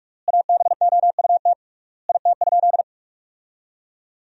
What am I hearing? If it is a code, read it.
Morse code, heard as a keyed tone, about 46 wpm, 700 Hz. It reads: ABOUT IT?
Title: 3x Word-Spacing - Sentences from Top 100 Words - 45wpm